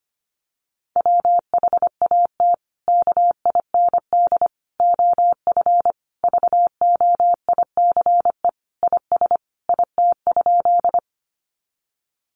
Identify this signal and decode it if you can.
Morse code, heard as a keyed tone, about 25 wpm, 700 Hz. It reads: WHAT KIND OF VOICE IS IT?
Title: Sentences from Top 500 Words 25wpm